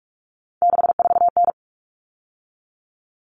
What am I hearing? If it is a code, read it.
Morse code, heard as a keyed tone, about 44 wpm, 700 Hz. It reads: 64N
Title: All Cut Numbers - 3-Digit 45wpm